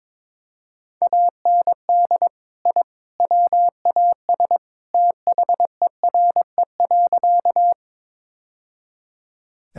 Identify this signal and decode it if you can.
Morse code, heard as a keyed tone, about 22 wpm, 700 Hz. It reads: AND I WAS THERE.
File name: practice / 22wpm / Sentences with 3-Word Phrases - 22wpm